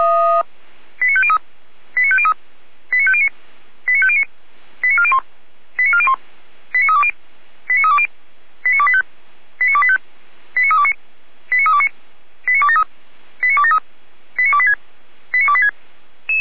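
Five-tone selective call (ZVEI), heard as quick runs of five tones, tones of about 70 ms. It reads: 89503 89503 89509 89509 89401 89401 89320 89320 89285 89285 89329 89329 89273 89273 89286 89286
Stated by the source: ZVEI five-tone call 89265 received by SDR.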